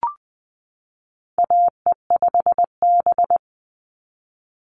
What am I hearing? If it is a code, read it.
Morse code, heard as a keyed tone, about 20 wpm, 700 Hz. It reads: AE5B
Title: NA SOTA Chaser Callsigns 20wpm